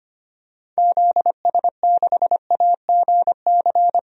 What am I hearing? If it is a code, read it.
Morse code, heard as a keyed tone, about 25 wpm, 700 Hz. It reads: ZS6AGC
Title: DXCC-100 Call Signs - 6 Characters - 25wpm